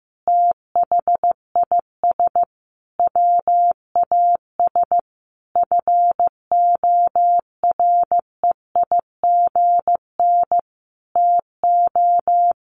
Morse code, heard as a keyed tone, about 15 wpm, 700 Hz. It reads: THIS WAS FOREIGN TO
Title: Sentences from Top 1400 Words 15wpm